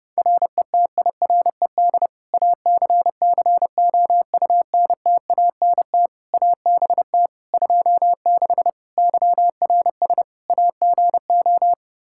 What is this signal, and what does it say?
Morse code, heard as a keyed tone, about 30 wpm, 700 Hz. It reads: RETIRED ACCOUNTANT ABT 26 YRS AGO
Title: Practice QSO - Exchange 3 - More Info - 30wpm